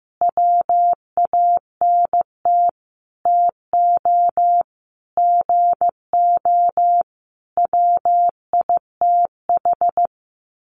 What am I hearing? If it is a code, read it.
Morse code, heard as a keyed tone, about 15 wpm, 700 Hz. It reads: WANT TO GO WITH
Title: Sentences from Top 1300 Words 15wpm